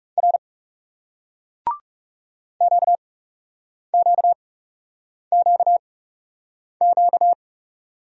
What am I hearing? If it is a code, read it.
Morse code, heard as a keyed tone, about 40 wpm, spaced as for 8 wpm, 700 Hz. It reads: R QQQQ